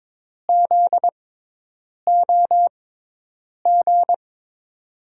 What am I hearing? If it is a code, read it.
Morse code, heard as a keyed tone, about 22 wpm, spaced as for 8 wpm, 700 Hz. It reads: ZOG
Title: USA Call Signs - 6 Characters - 6x Character Spacing - 22wpm